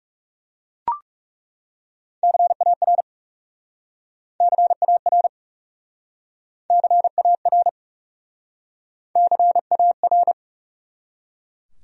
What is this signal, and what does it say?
Morse code, heard as a keyed tone, about 37 wpm, 700 Hz. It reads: CAR CAR CAR CAR
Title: Speed-Racing Top 300 Words 30wpm